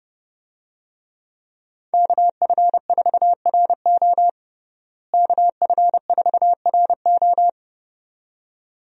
Morse code, heard as a keyed tone, about 32 wpm, 700 Hz. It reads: KF4RO KF4RO
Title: USA Call Signs - 5 Characters - Repeated 2x - 30wpm